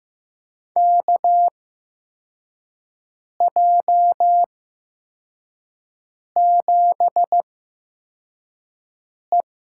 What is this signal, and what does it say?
Morse code, heard as a keyed tone, about 15 wpm, spaced as for 4 wpm, 700 Hz. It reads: KJ7E
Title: USA Call Signs - 8x Character Spacing - 15wpm